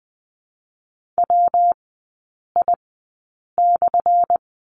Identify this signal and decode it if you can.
Morse code, heard as a keyed tone, about 20 wpm, 700 Hz. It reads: W I /
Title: 178 Lesson 29e - Slash - Three Characters 20wpm